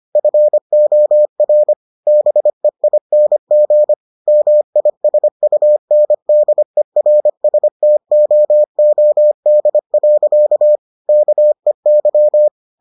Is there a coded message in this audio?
Yes: Morse code from 0.2 to 12.5 s, heard as a keyed tone, about 25 wpm, 600 Hz. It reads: FOR BEING MISUNDERSTOOD. KEY